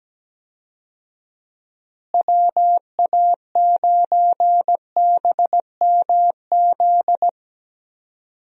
Morse code, heard as a keyed tone, about 17 wpm, 700 Hz. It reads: WA9BMZ
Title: USA Call Signs - 6 Characters - Repeated 2x - 17wpm